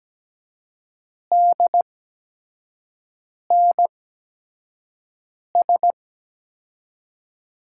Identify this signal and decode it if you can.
Morse code, heard as a keyed tone, about 17 wpm, spaced as for 5 wpm, 700 Hz. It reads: DNS
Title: USA Call Signs - 5 Characters - 8x Character Spacing - 17wpm